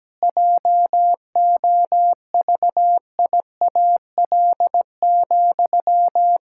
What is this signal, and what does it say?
Morse code, heard as a keyed tone, about 17 wpm, 700 Hz. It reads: JOVIAL,
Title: Pangram Challenge v2 - 17wpm